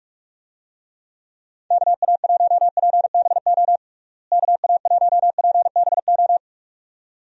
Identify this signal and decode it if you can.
Morse code, heard as a keyed tone, about 46 wpm, 700 Hz. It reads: KA1PBO KA1PBO
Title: USA Call Signs - 6 Characters - Repeated 2x - 45wpm